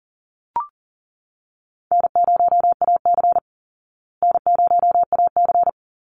Morse code, heard as a keyed tone, about 40 wpm, 700 Hz. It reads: N0AC N0AC
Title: Easy Call Signs Repeated 2x - 40wpm